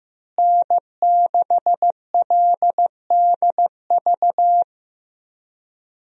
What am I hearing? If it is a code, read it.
Morse code, heard as a keyed tone, about 15 wpm, 700 Hz. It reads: N6LDV
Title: USA Call Signs - 5 Characters - Repeated 2x - 15wpm